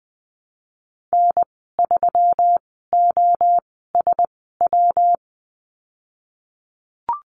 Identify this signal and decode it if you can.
Morse code, heard as a keyed tone, about 20 wpm, spaced as for 15 wpm, 700 Hz. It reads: N3OSW
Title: 147 Lesson 24g - Letter y - Call Signs 20wpm